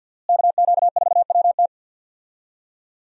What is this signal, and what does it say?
Morse code, heard as a keyed tone, about 52 wpm, 700 Hz. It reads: KQ4WT
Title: USA Call Signs - 5 Characters - Repeated 3x - 50wpm